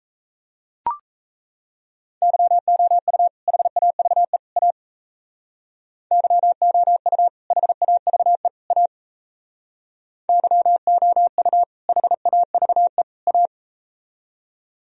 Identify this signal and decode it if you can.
Morse code, heard as a keyed tone, about 37 wpm, 700 Hz. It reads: YOU HAVE A YOU HAVE A YOU HAVE A